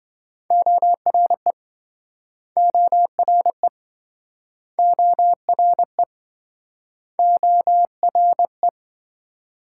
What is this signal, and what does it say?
Morse code, heard as a keyed tone, about 27 wpm, 700 Hz. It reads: ORE ORE ORE ORE